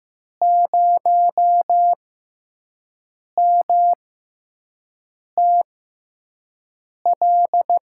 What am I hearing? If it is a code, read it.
Morse code, heard as a keyed tone, about 15 wpm, spaced as for 5 wpm, 700 Hz. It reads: 0MTL
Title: USA Call Signs - 6 Characters - 6x Character Spacing - 15wpm